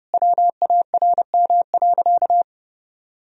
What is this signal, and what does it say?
Morse code, heard as a keyed tone, about 30 wpm, 700 Hz. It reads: WARM.
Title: Sentences from Top 1000 Words - Review Part 2 - 30wpm